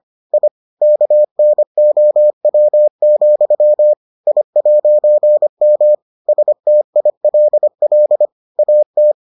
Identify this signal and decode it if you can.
Morse code, heard as a keyed tone, about 25 wpm, 600 Hz. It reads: I KNOW, I'M STILL AT